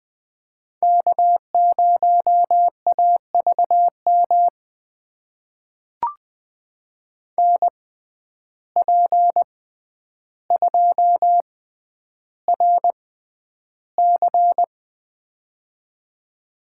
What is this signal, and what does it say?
Morse code, heard as a keyed tone, about 21 wpm, spaced as for 10 wpm, 700 Hz. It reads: K0AVM NP2RC
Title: USA Call Signs - 5 Characters - 6x Character Spacing - 20wpm